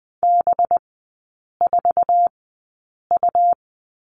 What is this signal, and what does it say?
Morse code, heard as a keyed tone, about 20 wpm, spaced as for 8 wpm, 690 Hz. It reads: B4U
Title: 191 Lesson 31e - Letter v - Three Characters 20wpm